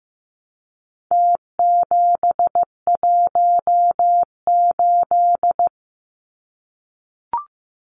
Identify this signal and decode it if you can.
Morse code, heard as a keyed tone, about 15 wpm, 700 Hz. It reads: T718
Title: Numbers - Contest Style - 4-Digit 15wpm